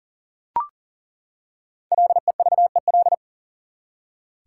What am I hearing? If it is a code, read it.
Morse code, heard as a keyed tone, about 40 wpm, 700 Hz. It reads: LEVEL
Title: Five Letter Words - 40wpm